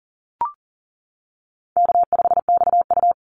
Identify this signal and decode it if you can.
Morse code, heard as a keyed tone, about 40 wpm, 700 Hz. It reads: K5XU
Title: Call Signs - 40wpm